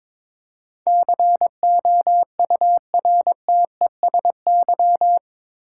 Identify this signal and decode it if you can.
Morse code, heard as a keyed tone, about 22 wpm, 700 Hz. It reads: COURTESY
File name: practice / 22wpm / Eight Letter Words - 22wpm